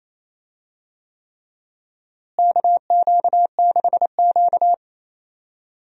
Morse code, heard as a keyed tone, about 28 wpm, 700 Hz. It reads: KQ6Q
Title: USA Call Signs - 4 Characters - Repeated 3x - 28wpm